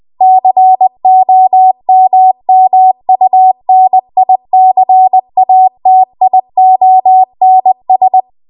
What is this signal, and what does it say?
Morse code, heard as a keyed tone, about 20 wpm, 750 Hz. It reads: COMMUNICATIONS
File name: Morse / 20WPM